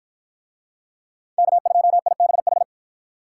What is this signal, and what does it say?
Morse code, heard as a keyed tone, about 53 wpm, 700 Hz. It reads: K2IBH